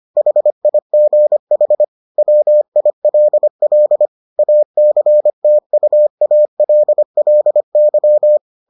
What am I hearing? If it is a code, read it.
Morse code, heard as a keyed tone, about 25 wpm, 600 Hz. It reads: HIGH WILL ACTUALLY